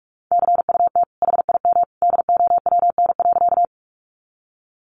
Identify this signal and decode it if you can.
Morse code, heard as a keyed tone, about 47 wpm, 700 Hz. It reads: CUT HIM DOWN.
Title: Sentences from Top 300 Words 45wpm